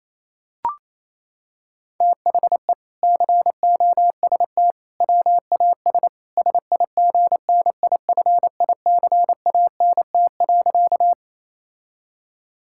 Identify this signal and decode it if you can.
Morse code, heard as a keyed tone, about 28 wpm, 700 Hz. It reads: THE COST WAS SIGNIFICANT.
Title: Sentences from Top 1000 Words - Review Part 6 - 28wpm